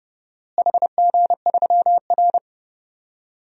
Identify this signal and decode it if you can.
Morse code, heard as a keyed tone, about 32 wpm, 700 Hz. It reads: HG3R
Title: DXCC-100 Call Signs - 4 Characters - Repeated 2x - 30wpm